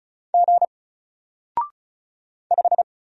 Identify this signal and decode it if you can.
Morse code, heard as a keyed tone, about 34 wpm, 700 Hz. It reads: G 5
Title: Single Character (Letter:Number:Punctuation:Prosign) Alt 35wpm